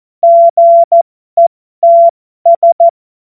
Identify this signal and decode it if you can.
Morse code, heard as a keyed tone, about 14 wpm, spaced as for 12 wpm, 680 Hz. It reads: GETS
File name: Morsekurs / Witze